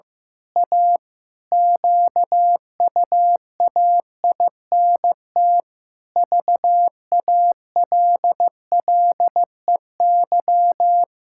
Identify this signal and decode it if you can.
Morse code, heard as a keyed tone, about 15 wpm, 700 Hz. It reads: A QUAINT VALLEY
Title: Pangram Challenge v2 - 15wpm